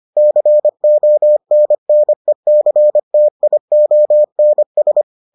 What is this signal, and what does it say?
Morse code, heard as a keyed tone, about 25 wpm, 600 Hz. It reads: CONNECTIONS